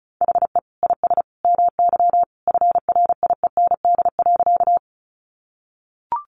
Morse code, heard as a keyed tone, about 34 wpm, 700 Hz. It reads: HE IS MY FRIEND.
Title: Sentences from Top 300 Words 35wpm